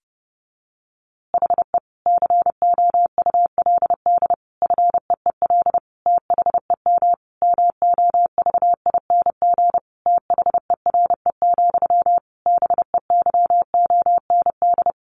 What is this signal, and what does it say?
Morse code, heard as a keyed tone, about 30 wpm, 700 Hz. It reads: HE COULD FEEL THEM MOVING THERE, BEYOND
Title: A Sound Of Thunder 30wpm